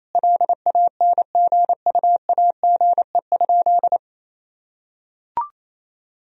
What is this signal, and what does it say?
Morse code, heard as a keyed tone, about 28 wpm, 700 Hz. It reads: LANGUAGE?
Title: Sentences from Top 1000 Words - Review Part 3 - 28wpm